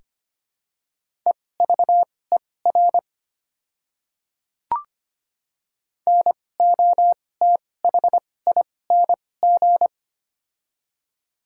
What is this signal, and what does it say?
Morse code, heard as a keyed tone, about 25 wpm, spaced as for 18 wpm, 700 Hz. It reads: EVER NOTHING
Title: 2x Character Spacing - Top 500 Words - 25wpm